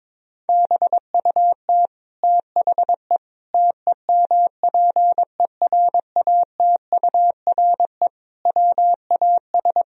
Morse code, heard as a keyed tone, about 22 wpm, 700 Hz. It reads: BUT THE TEMPERATURE WAS
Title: Sentences from Top 1000 Words - Review Part 4 - 22wpm